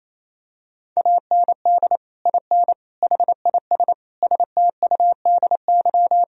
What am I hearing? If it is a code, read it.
Morse code, heard as a keyed tone, about 28 wpm, 700 Hz. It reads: AND IN HIS STUDY
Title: Sentences from Top 1000 Words - Review Part 1 - 28wpm